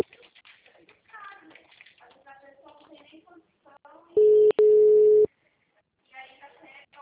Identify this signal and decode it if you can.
Ringback tone (Europe, 425 Hz), with speech.
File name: out- -2041- - - .20636